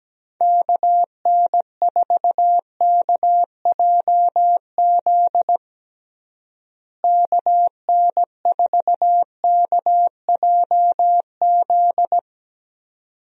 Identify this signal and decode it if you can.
Morse code, heard as a keyed tone, about 17 wpm, 700 Hz. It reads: KN4KJZ KN4KJZ
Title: USA Call Signs - 6 Characters - Repeated 2x - 17wpm